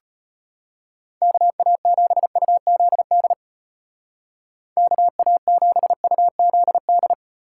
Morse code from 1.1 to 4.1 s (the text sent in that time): KA7UZD